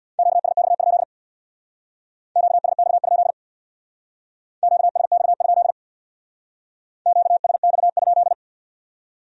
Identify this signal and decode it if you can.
Morse code, heard as a keyed tone, about 61 wpm, 700 Hz. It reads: QSX? QSX? QSX? QSX?
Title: Speed-Racing Q-Codes - 50wpm